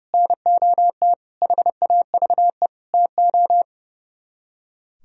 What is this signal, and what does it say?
Morse code, heard as a keyed tone, about 40 wpm, 700 Hz. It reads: NOT HAVE TO